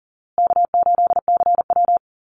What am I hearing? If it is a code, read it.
Morse code, heard as a keyed tone, about 39 wpm, 700 Hz. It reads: K8CW
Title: USA Call Signs Repeated 3x 40wpm